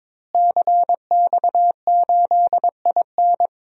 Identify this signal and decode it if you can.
Morse code, heard as a keyed tone, about 22 wpm, 700 Hz. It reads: CX8IN
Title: DXCC-100 Call Signs - 5 Characters - Repeated 2x - 22wpm